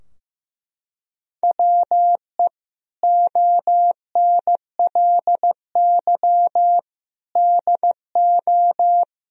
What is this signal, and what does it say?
Morse code, heard as a keyed tone, about 15 wpm, 700 Hz. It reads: WE ONLY DO
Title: Sentences from Top 1000 Words - Review Part 4 - 15wpm